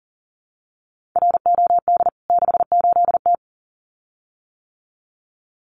Morse code, heard as a keyed tone, about 40 wpm, 700 Hz. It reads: ROD 68T
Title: CWT Exchange 40wpm